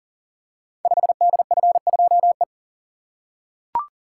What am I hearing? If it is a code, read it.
Morse code, heard as a keyed tone, about 42 wpm, 700 Hz. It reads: 5DF2E
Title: Code Groups - 5 Letter:Numbers - 40wpm